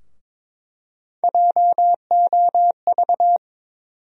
Morse code, heard as a keyed tone, about 22 wpm, 700 Hz. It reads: JOV
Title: USA Call Sign Postfixes - 3 Letters 22wpm